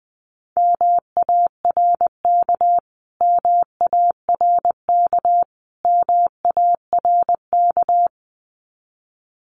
Morse code, heard as a keyed tone, about 20 wpm, 700 Hz. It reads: MARK MARK MARK
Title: Top 300 Words Repeated 20wpm